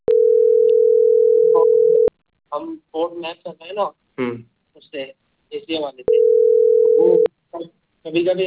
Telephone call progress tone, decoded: ringback tone